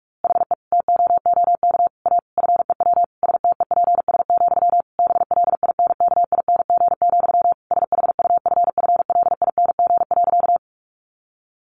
Morse code, heard as a keyed tone, about 44 wpm, 700 Hz. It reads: HE TOOK A FEW STEPS, BLINKING, SHUFFLING.
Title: A Sound Of Thunder 45wpm